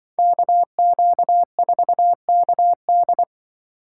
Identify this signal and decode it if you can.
Morse code, heard as a keyed tone, about 25 wpm, 700 Hz. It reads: KQ4KD
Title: Speed Racing - USA 2x2 Call Signs 20wpm